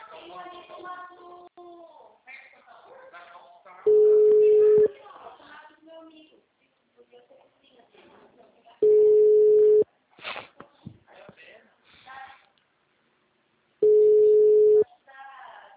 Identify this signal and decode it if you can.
Ringback tone (Europe, 425 Hz), with speech.
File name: out- -2019- - - .259